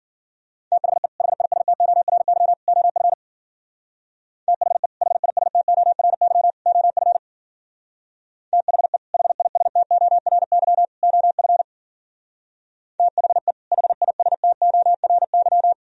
Morse code, heard as a keyed tone, about 50 wpm, 700 Hz. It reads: THE HISTORY OF THE HISTORY OF THE HISTORY OF THE HISTORY